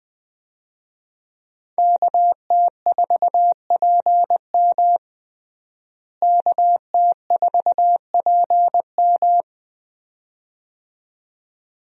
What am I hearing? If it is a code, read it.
Morse code, heard as a keyed tone, about 21 wpm, 700 Hz. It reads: KT4PM KT4PM
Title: USA Call Signs - 5 Characters - Repeated 2x - 20wpm